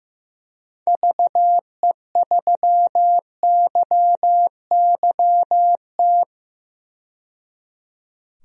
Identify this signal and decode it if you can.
Morse code, heard as a keyed tone, about 15 wpm, 700 Hz. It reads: VE3YYT
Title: DXCC-100 Call Signs - 6 Characters - 15wpm